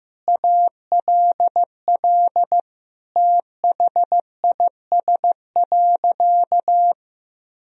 Morse code, heard as a keyed tone, about 15 wpm, 700 Hz. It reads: ALL THIS.
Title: Sentences from Top 1000 Words - Review Part 1 - 15wpm